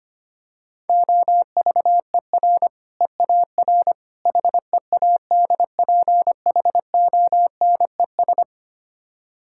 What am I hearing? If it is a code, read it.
Morse code, heard as a keyed tone, about 25 wpm, 700 Hz. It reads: OVER EAR HEADPHONES
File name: practice / 25wpm / Difficult Words - Speed-Racing with Hints - 25wpm